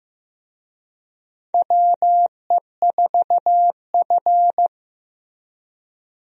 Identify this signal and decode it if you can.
Morse code, heard as a keyed tone, about 15 wpm, 700 Hz. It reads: WE4F
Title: USA Call Signs - 4 Characters - Repeated 2x - 15wpm